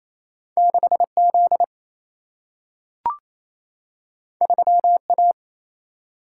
Morse code, heard as a keyed tone, about 28 wpm, 700 Hz. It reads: 6Z 3A
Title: Single-Letter Single-Number 28wpm